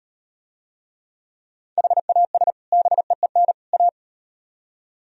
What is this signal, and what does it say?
Morse code, heard as a keyed tone, about 38 wpm, 700 Hz. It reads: HAS BEEN A